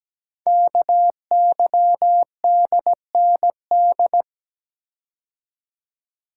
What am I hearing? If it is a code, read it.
Morse code, heard as a keyed tone, about 17 wpm, 700 Hz. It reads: KYDND
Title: Code Groups - 5 Letters - 17wpm